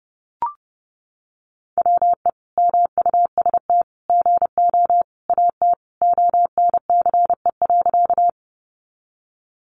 Morse code, heard as a keyed tone, about 28 wpm, 700 Hz. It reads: WE MUST GO AT ONCE.
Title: Sentences from Top 300 Words 30wpm